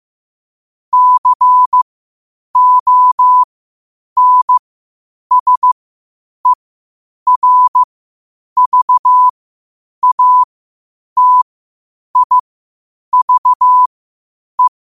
Morse code, heard as a keyed tone, about 15 wpm, spaced as for 8 wpm, 1000 Hz. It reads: CONSERVATIVE
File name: qotdfiles / 8wpm